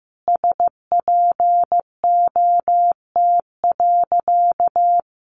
Morse code, heard as a keyed tone, about 14 wpm, 700 Hz. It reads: SPOT.
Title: Sentences from Top 1200 Words 15wpm